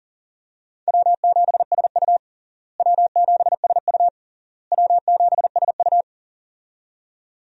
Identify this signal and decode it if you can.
Morse code, heard as a keyed tone, about 40 wpm, 700 Hz. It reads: W7SU W7SU W7SU
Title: USA Call Signs - 4 Characters - Repeated 3x - 40wpm